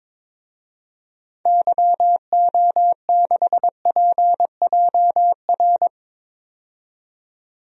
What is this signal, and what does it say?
Morse code, heard as a keyed tone, about 22 wpm, 700 Hz. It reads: YO6PJR
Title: DXCC-100 Call Signs - 6 Characters - 22wpm